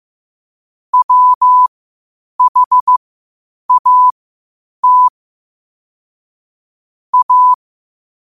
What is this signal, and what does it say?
Morse code, heard as a keyed tone, about 15 wpm, spaced as for 8 wpm, 1000 Hz. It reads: WHAT A
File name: qotdfiles / 8wpm